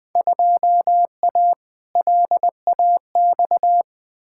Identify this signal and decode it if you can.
Morse code, heard as a keyed tone, about 21 wpm, 700 Hz. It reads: 2A LAX
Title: Field Day Exchanges - 20wpm